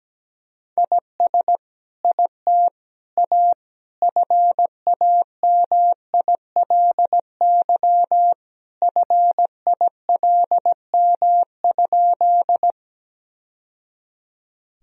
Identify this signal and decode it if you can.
Morse code, heard as a keyed tone, about 17 wpm, 700 Hz. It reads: IS IT A FAMILY FILM?
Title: Sentences from Top 1000 Words - Review Part 3 - 17wpm